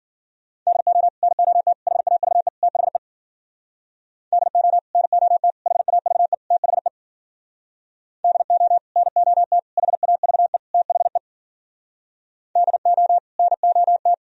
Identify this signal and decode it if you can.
Morse code, heard as a keyed tone, about 52 wpm, 700 Hz. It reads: DO NOT HAVE THE DO NOT HAVE THE DO NOT HAVE THE DO NOT